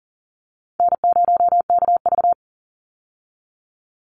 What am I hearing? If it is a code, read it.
Morse code, heard as a keyed tone, about 40 wpm, 700 Hz. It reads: N0KV
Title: Easy Call Signs - 40wpm